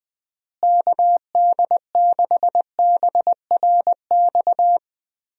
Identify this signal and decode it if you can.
Morse code, heard as a keyed tone, about 20 wpm, 700 Hz. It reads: KD6BRX
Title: USA Call Signs - 6 Characters - Repeated 3x - 20wpm